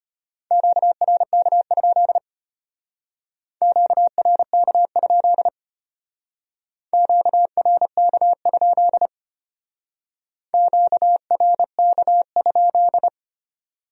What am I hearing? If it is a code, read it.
Morse code, heard as a keyed tone, about 32 wpm, 700 Hz. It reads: QRK? QRK? QRK? QRK?